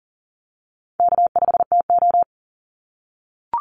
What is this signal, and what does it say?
Morse code, heard as a keyed tone, about 40 wpm, 700 Hz. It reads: K5TO
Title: Call Signs - 40wpm